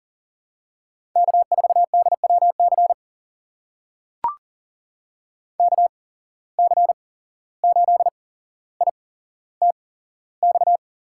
Morse code, heard as a keyed tone, about 40 wpm, spaced as for 16 wpm, 700 Hz. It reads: K4DWC KC8ITX
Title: USA Call Signs - 8x Character Spacing - 40wpm